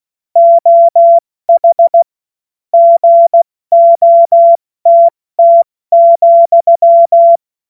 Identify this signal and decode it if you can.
Morse code, heard as a keyed tone, about 16 wpm, 680 Hz. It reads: OH GOTT,
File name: Morsekurs / Witze / Witz 17. August 2021, 70 F80